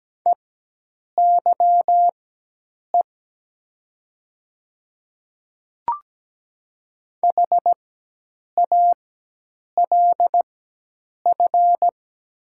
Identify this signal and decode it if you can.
Morse code, heard as a keyed tone, about 17 wpm, spaced as for 8 wpm, 700 Hz. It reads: EYE HALF
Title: 4x Character Spacing - Top 500 Words - 17wpm